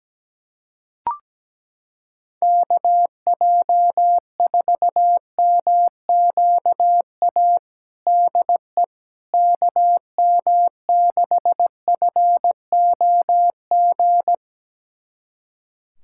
Morse code, heard as a keyed tone, about 17 wpm, 700 Hz. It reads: KJ4MQA DE KM6FOG
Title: USA Callsign DE Callsign - 6 Characters - 17wpm